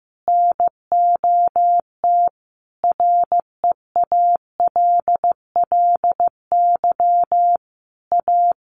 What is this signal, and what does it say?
Morse code, heard as a keyed tone, about 15 wpm, 700 Hz. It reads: NOT REALLY A
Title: Sentences from Top 1500 Words 15wpm